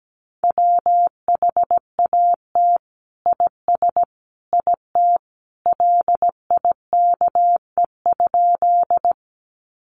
Morse code, heard as a keyed tone, about 17 wpm, 700 Hz. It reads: WHAT IS IT LIKE?
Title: Sentences from Top 100 Words 17wpm